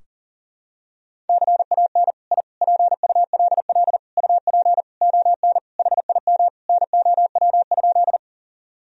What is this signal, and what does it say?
Morse code, heard as a keyed tone, about 40 wpm, 700 Hz. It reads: CAN I PULL UP ON HIM NOW?
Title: Sentences from Top 1000 Words - Review Part 2 - 40wpm